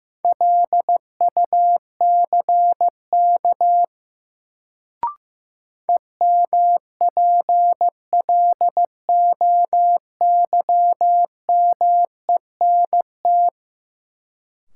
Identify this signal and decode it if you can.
Morse code, heard as a keyed tone, about 15 wpm, 700 Hz. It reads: LUCK EMPLOYMENT
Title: Sentences from Top 1500 Words - New Word Drill Alt - 15wpm